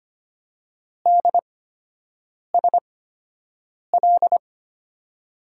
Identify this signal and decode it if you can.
Morse code, heard as a keyed tone, about 25 wpm, spaced as for 7 wpm, 700 Hz. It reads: DSL
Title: USA Call Signs - 5 Characters - 8x Character Spacing - 25wpm